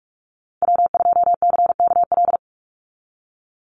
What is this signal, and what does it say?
Morse code, heard as a keyed tone, about 44 wpm, 700 Hz. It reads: W2CKL
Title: NA SOTA Activator Callsigns 45wpm